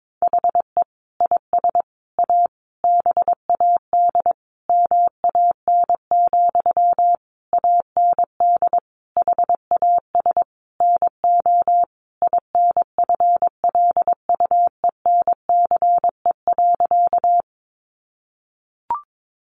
Morse code, heard as a keyed tone, about 22 wpm, 700 Hz. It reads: HE IS A BAD MAN, AND HAS NO INFLUENCE.
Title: Sentences from Top 1100 Words 22wpm